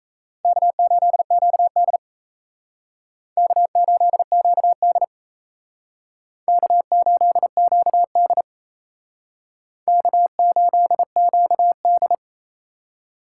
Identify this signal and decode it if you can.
Morse code, heard as a keyed tone, about 35 wpm, 700 Hz. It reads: K8QD K8QD K8QD K8QD